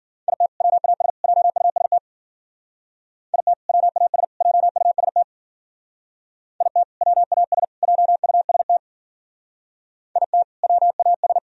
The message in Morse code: IT WAS JUST IT WAS JUST IT WAS JUST IT WAS

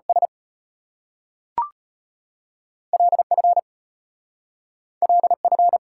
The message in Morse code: S LF LF

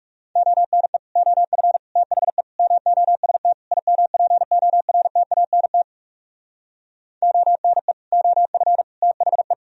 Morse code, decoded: ONE OF THE MOST IMPORTANT ONE OF THE

